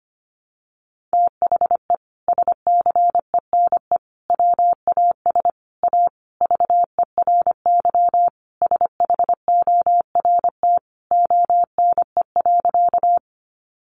Morse code, heard as a keyed tone, about 25 wpm, 700 Hz. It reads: THE SCENE WAS A VERY SHORT ONE.
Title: Sentences from Top 1100 Words 25wpm